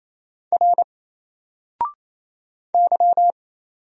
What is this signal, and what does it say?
Morse code, heard as a keyed tone, about 27 wpm, 700 Hz. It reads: R Y
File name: Single-Letter-Number-28wpm